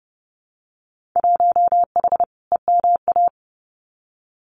Morse code, heard as a keyed tone, about 30 wpm, 700 Hz. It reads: 1H EMA
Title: Winter Field Day Exchanges 30wpm